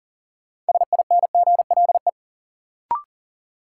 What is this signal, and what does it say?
Morse code, heard as a keyed tone, about 40 wpm, 700 Hz. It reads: SINGLE
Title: Sentences from Top 700 Words - New Word Drill - 40wpm